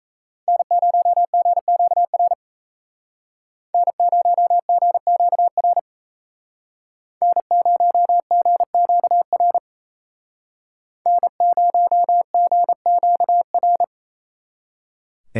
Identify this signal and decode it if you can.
Morse code, heard as a keyed tone, about 36 wpm, 700 Hz. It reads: N0GQR N0GQR N0GQR N0GQR